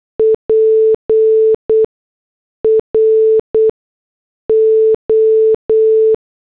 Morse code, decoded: PRO